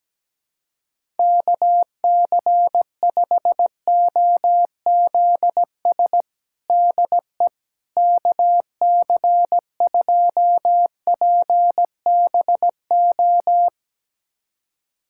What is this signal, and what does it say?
Morse code, heard as a keyed tone, about 17 wpm, 700 Hz. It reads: KC5OZS DE KC2PBO
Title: USA Callsign DE Callsign 17wpm